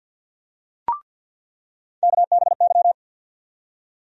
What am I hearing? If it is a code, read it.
Morse code, heard as a keyed tone, about 53 wpm, 700 Hz. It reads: KBY
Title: USA Call Sign Postfixes - 3 Letters 50wpm